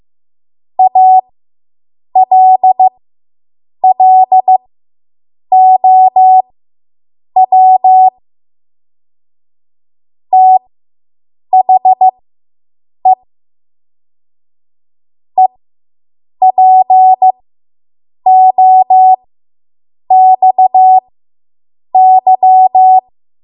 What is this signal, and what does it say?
Morse code, heard as a keyed tone, about 15 wpm, spaced as for 7 wpm, 750 Hz. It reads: ALLOW THE EPOXY